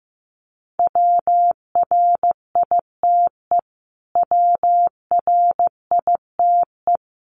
Morse code, heard as a keyed tone, about 15 wpm, 700 Hz. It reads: WRITE WRITE
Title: Top 500 Words Repeated 15wpm